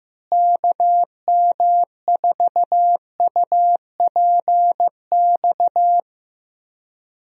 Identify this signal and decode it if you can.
Morse code, heard as a keyed tone, about 15 wpm, 700 Hz. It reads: KM4UPX
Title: USA Call Signs - 6 Characters - Repeated 2x - 15wpm